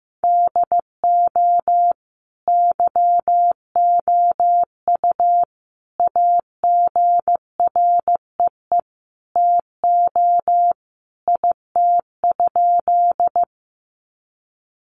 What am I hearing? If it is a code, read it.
Morse code, heard as a keyed tone, about 15 wpm, 700 Hz. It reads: DO YOU AGREE TO IT?
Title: Sentences from Top 1100 Words 15wpm